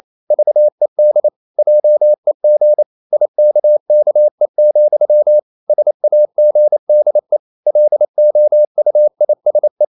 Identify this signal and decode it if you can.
Morse code, heard as a keyed tone, about 28 wpm, 600 Hz. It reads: VED JEG IKKE, SAGDE LOUISE
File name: Ferie QRQ 28wpm